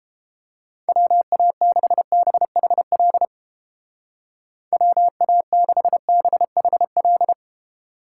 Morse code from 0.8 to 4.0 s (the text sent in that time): WA6BHL